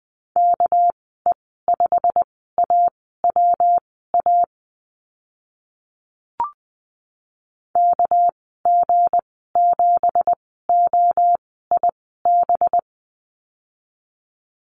Morse code, heard as a keyed tone, about 20 wpm, spaced as for 14 wpm, 700 Hz. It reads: KE5AWA KG7OIB
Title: 245 Lesson 39g - Letter z - Call Signs 20wpm